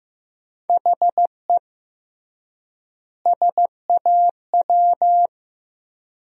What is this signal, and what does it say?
Morse code, heard as a keyed tone, about 16 wpm, 700 Hz. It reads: HE SAW
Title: 3x Word-Spacing - Sentences from Top 300 Words - 15wpm